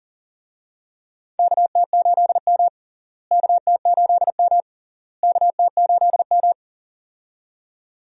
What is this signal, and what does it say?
Morse code, heard as a keyed tone, about 40 wpm, 700 Hz. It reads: KT8M KT8M KT8M
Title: USA Call Signs - 4 Characters - Repeated 3x - 40wpm